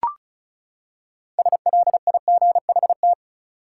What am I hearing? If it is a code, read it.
Morse code, heard as a keyed tone, about 36 wpm, 700 Hz. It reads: SLIGHT
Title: Six Letter Words - 35wpm